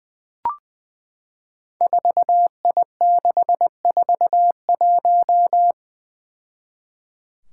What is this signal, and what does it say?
Morse code, heard as a keyed tone, about 20 wpm, 700 Hz. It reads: 4I641
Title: Five Challenging Letter-Number Combinations 20wpm